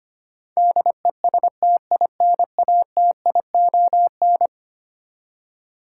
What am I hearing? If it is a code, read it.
Morse code, heard as a keyed tone, about 25 wpm, 700 Hz. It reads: DESTINATION
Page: morse-code-ninja-practice-25wpm